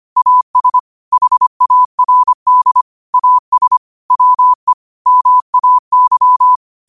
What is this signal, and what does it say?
Morse code, heard as a keyed tone, about 25 wpm, 1000 Hz. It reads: AS HARD AS WE MAY